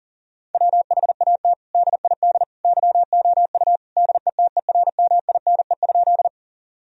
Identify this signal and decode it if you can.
Morse code, heard as a keyed tone, about 40 wpm, 700 Hz. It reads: WHAT DID YOU DETERMINE?
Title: Sentences from Top 1000 Words - Review Part 4 - 40wpm